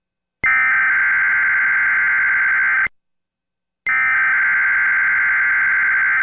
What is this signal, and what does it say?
Emergency Alert System (SAME) tones (header bursts), heard as harsh buzzing bursts. It is heard from 0.5 to 5.5 s.